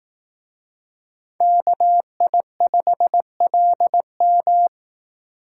Morse code, heard as a keyed tone, about 18 wpm, 700 Hz. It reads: KI5LM
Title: Speed Racing - USA 2x2 Call Signs 15wpm